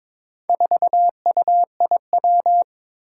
Morse code, heard as a keyed tone, about 22 wpm, 700 Hz. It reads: 4UIW
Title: Four Challenging Letter-Number Combinations 22wpm